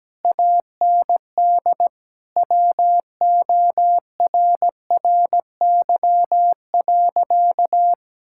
Morse code, heard as a keyed tone, about 17 wpm, 700 Hz. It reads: AND WORRY.
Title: Pangram Challenge v2 - 17wpm